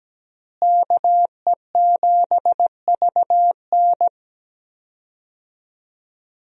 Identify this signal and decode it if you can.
Morse code, heard as a keyed tone, about 17 wpm, 700 Hz. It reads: KE7VN
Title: USA Call Signs - 5 Characters - Repeated 2x - 17wpm